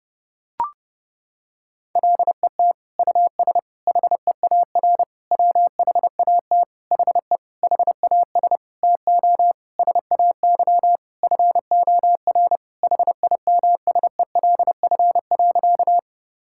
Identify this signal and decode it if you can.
Morse code, heard as a keyed tone, about 30 wpm, 700 Hz. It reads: LET US HEAR WHAT HE HAS TO SAY FOR HIMSELF.
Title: Sentences from Top 1600 Words 30wpm